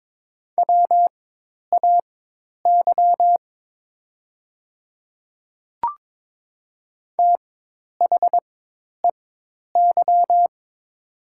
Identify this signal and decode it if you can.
Morse code, heard as a keyed tone, about 22 wpm, spaced as for 10 wpm, 700 Hz. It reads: WAY THEY
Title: 4x Character Spacing - Top 300 Words - 22wpm